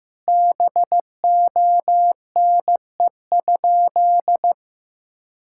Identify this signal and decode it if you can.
Morse code, heard as a keyed tone, about 15 wpm, 700 Hz. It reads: BONE?
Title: Sentences from Top 1600 Words 15wpm